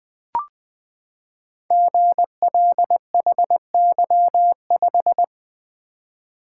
Morse code, heard as a keyed tone, about 21 wpm, 700 Hz. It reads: GLHY5
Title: Code Groups - 5 Letter:Numbers - 20wpm